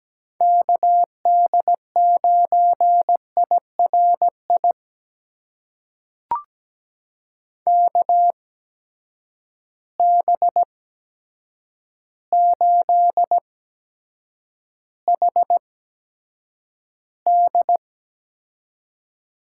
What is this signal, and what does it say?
Morse code, heard as a keyed tone, about 17 wpm, spaced as for 8 wpm, 700 Hz. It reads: KD9IRI KB8HD
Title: USA Call Signs - 6 Characters - 8x Character Spacing - 17wpm